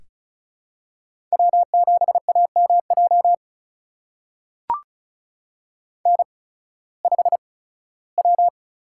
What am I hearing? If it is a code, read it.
Morse code, heard as a keyed tone, about 36 wpm, spaced as for 19 wpm, 700 Hz. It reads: W7AMJ N5W
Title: USA Call Signs - 5 Characters - 8x Character Spacing - 35wpm